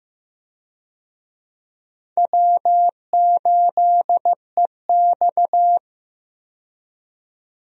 Morse code, heard as a keyed tone, about 15 wpm, 700 Hz. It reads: W8EX
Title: USA Call Signs - 4 Characters - 15wpm